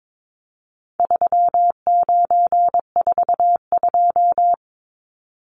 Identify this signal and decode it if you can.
Morse code, heard as a keyed tone, about 22 wpm, 700 Hz. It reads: 3942
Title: Numbers - 4-Digit 22wpm